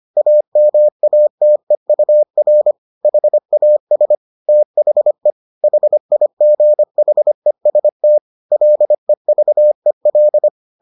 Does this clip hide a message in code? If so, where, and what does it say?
Yes: Morse code from 0.2 to 10.5 s, heard as a keyed tone, about 25 wpm, 600 Hz. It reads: AMATEUR HAS THE HIGHEST LEVEL